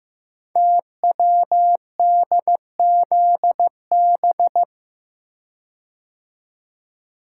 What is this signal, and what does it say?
Morse code, heard as a keyed tone, about 15 wpm, 700 Hz. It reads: TWDZB
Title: Code Groups - 5 Letters - 15wpm